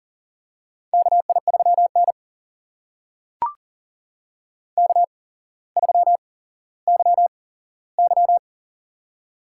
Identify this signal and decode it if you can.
Morse code, heard as a keyed tone, about 40 wpm, spaced as for 17 wpm, 690 Hz. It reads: KI3N K3YY
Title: USA Call Signs - 4 Characters - 8x Character Spacing - 40wpm